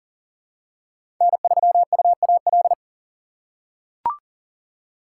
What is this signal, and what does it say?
Morse code, heard as a keyed tone, about 40 wpm, 700 Hz. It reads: N3UAL
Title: USA Call Signs - 5 Characters - 6x Character Spacing - 40wpm